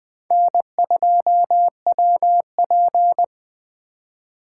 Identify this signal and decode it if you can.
Morse code, heard as a keyed tone, about 20 wpm, 700 Hz. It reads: N2WP
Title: USA Call Signs - 4 Characters - 3x Character Spacing - 20wpm